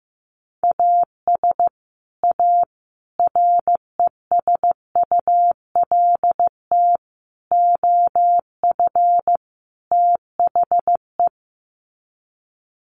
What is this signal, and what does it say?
Morse code, heard as a keyed tone, about 15 wpm, 700 Hz. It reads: AS A RESULT OF THE
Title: Sets of 5 Words 15wpm